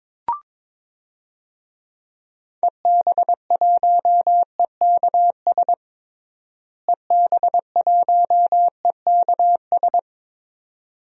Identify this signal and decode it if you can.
Morse code, heard as a keyed tone, about 22 wpm, 700 Hz. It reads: EB1EKS EB1EKS
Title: DXCC-100 Call Signs - 6 Characters - Repeated 2x - 22wpm